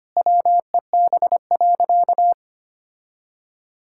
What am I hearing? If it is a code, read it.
Morse code, heard as a keyed tone, about 25 wpm, 700 Hz. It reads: WEB.
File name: practice / 25wpm / All-Age Appropriate Jokes - 25wpm